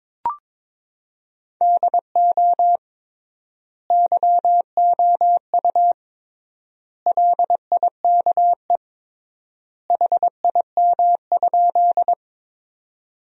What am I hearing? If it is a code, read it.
Morse code, heard as a keyed tone, about 21 wpm, 700 Hz. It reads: DO YOU LIKE HIM?
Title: 3x Word-Spacing - Sentences from Top 100 Words - 22wpm